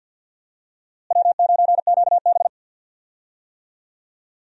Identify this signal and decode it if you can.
Morse code, heard as a keyed tone, about 52 wpm, 700 Hz. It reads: W9QB
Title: USA Call Signs - 4 Characters - Repeated 2x - 50wpm